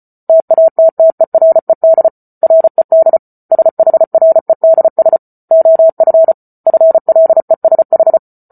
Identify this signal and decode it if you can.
Morse code, heard as a keyed tone, about 34 wpm, 650 Hz. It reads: TATTERED RED SHREDS OF FLESH
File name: Book Two / 08-Chapter 8 - 34 WPM